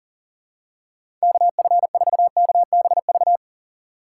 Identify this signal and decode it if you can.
Morse code, heard as a keyed tone, about 40 wpm, 700 Hz. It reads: KF4KBV
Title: USA Call Signs - 6x Character Spacing - 40wpm